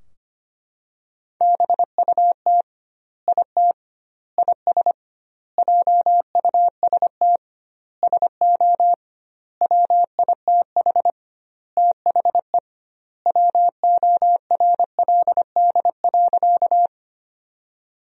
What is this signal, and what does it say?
Morse code, heard as a keyed tone, about 25 wpm, 700 Hz. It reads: BUT IT IS JUST SO WITH THE WORLD.
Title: 2x Word-Spacing - Sentences from Top 200 Words - 25wpm